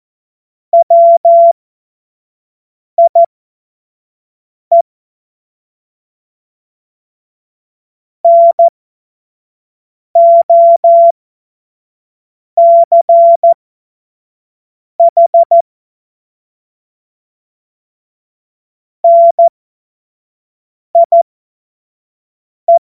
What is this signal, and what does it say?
Morse code, heard as a keyed tone, about 14 wpm, spaced as for 5 wpm, 680 Hz. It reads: WIE NOCH NIE
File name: Morsekurs / Witze / Witz 02. Februar 2021, 25 F70